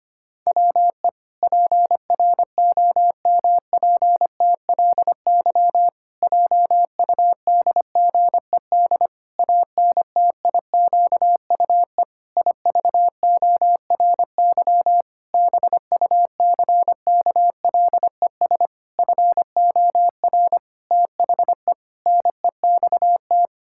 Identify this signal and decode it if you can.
Morse code, heard as a keyed tone, about 25 wpm, 700 Hz. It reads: WE PROMPTLY JUDGED ANTIQUE IVORY BUCKLES FOR THE NEXT
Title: Pangram Challenge v2 - 25wpm